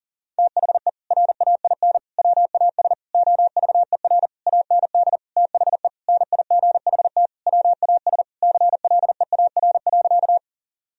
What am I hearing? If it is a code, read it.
Morse code, heard as a keyed tone, about 40 wpm, 700 Hz. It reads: THE RAIN WAS OVER AND THE NIGHT WAS CLEAR.
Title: Sentences from Top 1000 Words - Review Part 1 - 40wpm